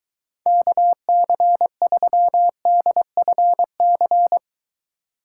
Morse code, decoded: KC3DFC

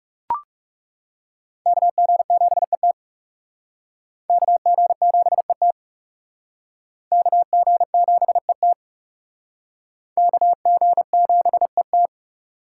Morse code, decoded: KG7ET KG7ET KG7ET KG7ET

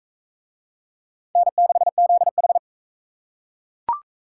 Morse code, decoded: N6ZH

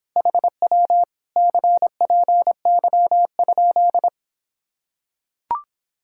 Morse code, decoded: HW CPY?

